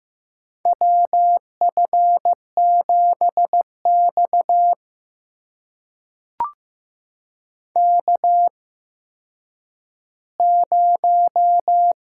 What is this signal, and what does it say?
Morse code, heard as a keyed tone, about 15 wpm, spaced as for 9 wpm, 700 Hz. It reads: WF7X K0
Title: USA Call Signs - 4 Characters - 8x Character Spacing - 15wpm